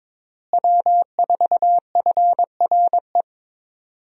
Morse code, heard as a keyed tone, about 22 wpm, 700 Hz. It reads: W4FRE
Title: USA Call Signs - 5 Characters - Repeated 3x - 22wpm